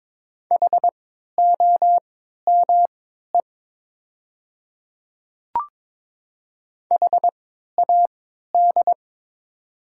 Morse code, heard as a keyed tone, about 22 wpm, spaced as for 13 wpm, 700 Hz. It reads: HOME HAD
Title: 3x Character Spacing - Top 500 Words - 22wpm